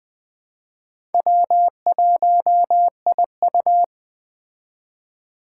Morse code, heard as a keyed tone, about 21 wpm, 700 Hz. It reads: W1IU
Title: Speed Racing - USA Call Signs 17wpm